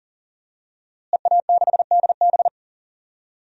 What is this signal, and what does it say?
Morse code, heard as a keyed tone, about 38 wpm, 700 Hz. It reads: EA6DB
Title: DXCC-100 Call Signs - 5 Characters - Repeated 3x - 40wpm